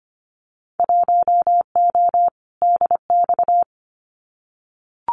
Morse code, heard as a keyed tone, about 25 wpm, 700 Hz. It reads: 1O DX
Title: Winter Field Day Exchanges 25wpm